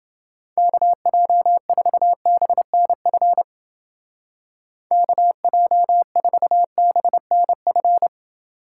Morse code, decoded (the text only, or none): KJ4BNF KJ4BNF